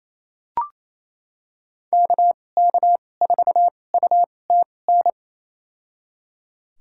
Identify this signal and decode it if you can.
Morse code, heard as a keyed tone, about 28 wpm, spaced as for 20 wpm, 700 Hz. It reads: KK4UTN
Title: USA Call Signs - 2x Character Spacing - 28wpm